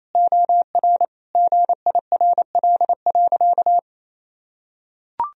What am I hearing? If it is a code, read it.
Morse code, heard as a keyed tone, about 28 wpm, 690 Hz. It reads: OR GIRL.
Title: Sentences from Top 1000 Words - Review Part 3 - 28wpm